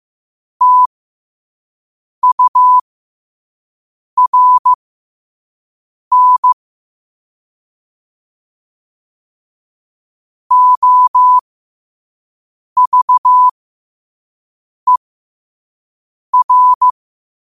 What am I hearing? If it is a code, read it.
Morse code, heard as a keyed tone, about 15 wpm, spaced as for 5 wpm, 1000 Hz. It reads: TURN OVER
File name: qotdfiles / 5wpm